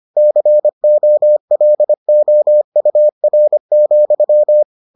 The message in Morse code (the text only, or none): COLOUR,